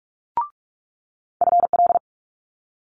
Morse code, heard as a keyed tone, about 46 wpm, 700 Hz. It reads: FL
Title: Multi-letter Phonemes Quick Fire 45wpm